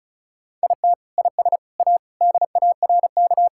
Morse code, heard as a keyed tone, about 36 wpm, 700 Hz. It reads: IT IS A DARK
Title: Sentences from Top 1000 Words - Review Part 5 - 35wpm